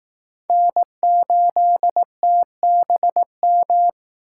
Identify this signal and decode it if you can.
Morse code, heard as a keyed tone, about 20 wpm, 700 Hz. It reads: N8TBM